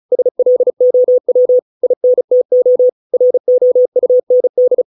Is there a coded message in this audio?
Yes: Morse code from 0.1 to 4.8 s, heard as a keyed tone, about 34 wpm, 500 Hz. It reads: SLOW INTO ROUND